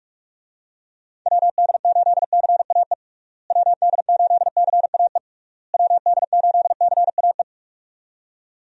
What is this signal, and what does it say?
Morse code, heard as a keyed tone, about 46 wpm, 700 Hz. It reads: WD8CAE WD8CAE WD8CAE
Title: USA Call Signs - 6 Characters - Repeated 3x - 45wpm